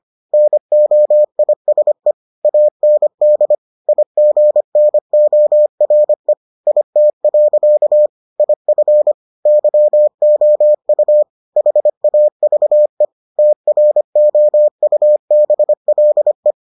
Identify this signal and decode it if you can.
Morse code, heard as a keyed tone, about 25 wpm, 600 Hz. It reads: NOISE AND IGNORE IT. IF YOU HAVE TROUBLE